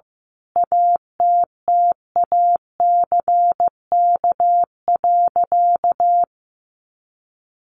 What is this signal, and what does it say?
Morse code, heard as a keyed tone, about 15 wpm, 700 Hz. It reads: ATTACK.
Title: Sentences from Top 1500 Words 15wpm